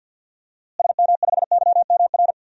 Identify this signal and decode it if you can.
Morse code, heard as a keyed tone, about 47 wpm, 700 Hz. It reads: SM5YMR
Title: DXCC-100 Call Signs - 6 Characters - Repeated 3x - 50wpm